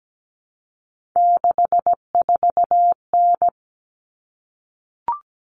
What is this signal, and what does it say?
Morse code, heard as a keyed tone, about 17 wpm, 700 Hz. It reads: 64N
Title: Common Cut Numbers - 3-Digit 17wpm